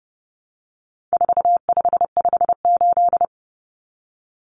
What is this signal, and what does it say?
Morse code, heard as a keyed tone, about 30 wpm, 700 Hz. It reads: 4558
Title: Numbers - 4-Digit 30wpm